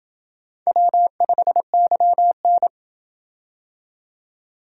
Morse code, heard as a keyed tone, about 27 wpm, 700 Hz. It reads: W5YN